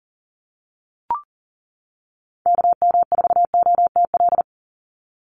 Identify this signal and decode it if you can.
Morse code, heard as a keyed tone, about 40 wpm, 700 Hz. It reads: KM4OTL
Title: USA Call Signs 40wpm